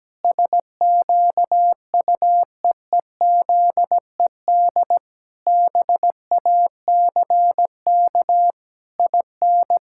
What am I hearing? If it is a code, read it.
Morse code, heard as a keyed tone, about 17 wpm, 700 Hz. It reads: SQUEEZED BACK IN
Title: Pangram Challenge v2 - 17wpm